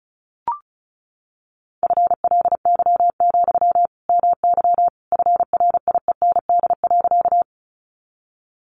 Morse code, heard as a keyed tone, about 34 wpm, 700 Hz. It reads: FLY, MY FRIEND.
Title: Sentences from Top 500 Words 35wpm